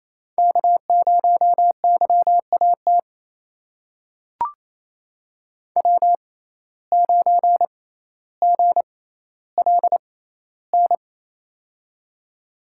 Morse code, heard as a keyed tone, about 28 wpm, spaced as for 14 wpm, 700 Hz. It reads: K0YAT W9GLN
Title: USA Call Signs - 5 Characters - 6x Character Spacing - 28wpm